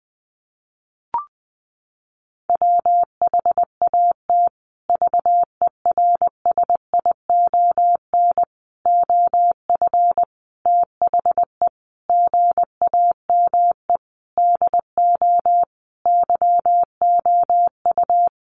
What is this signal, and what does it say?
Morse code, heard as a keyed tone, about 20 wpm, 700 Hz. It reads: WHAT VERSION OF THE GAME DO YOU
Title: Sentences from Top 1400 Words 20wpm